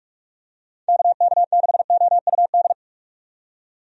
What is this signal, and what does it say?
Morse code, heard as a keyed tone, about 46 wpm, 700 Hz. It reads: KK6OUD
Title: USA Call Signs - 6 Characters - 45wpm